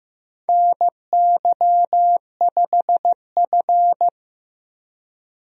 Morse code, heard as a keyed tone, about 14 wpm, 700 Hz. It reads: NY5F
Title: USA Call Signs - 4 Characters - Repeated 3x - 15wpm